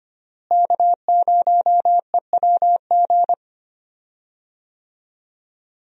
Morse code, heard as a keyed tone, about 25 wpm, 700 Hz. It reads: K0EWG
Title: USA Call Signs - 5 Characters - Repeated 3x - 25wpm